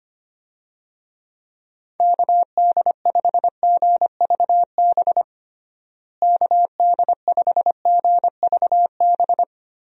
Morse code, heard as a keyed tone, about 25 wpm, 700 Hz. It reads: KD5GVB KD5GVB
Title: USA Call Signs - 6 Characters - Repeated 2x - 25wpm